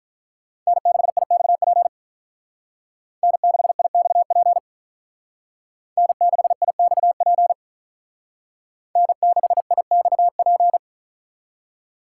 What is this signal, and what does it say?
Morse code, heard as a keyed tone, about 42 wpm, 700 Hz. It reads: N6IXP N6IXP N6IXP N6IXP
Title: Speed Racing - USA 1x3 Call Signs 35wpm